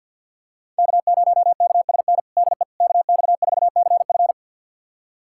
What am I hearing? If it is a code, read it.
Morse code, heard as a keyed tone, about 47 wpm, 690 Hz. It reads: K0KSN DE KX4CF
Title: USA Callsign DE Callsign 50wpm